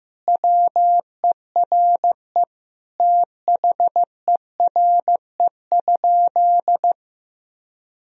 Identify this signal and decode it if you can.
Morse code, heard as a keyed tone, about 15 wpm, 700 Hz. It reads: WERE THERE?
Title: Sentences from Top 1000 Words - Review Part 1 - 15wpm